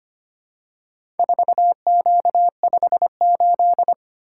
Morse code, heard as a keyed tone, about 25 wpm, 700 Hz. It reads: 4Q58
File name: practice / 25wpm / Four Letter-Number Combinations - 25wpm